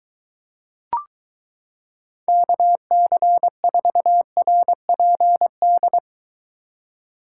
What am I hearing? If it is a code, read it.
Morse code, heard as a keyed tone, about 22 wpm, 700 Hz. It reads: KC4RPD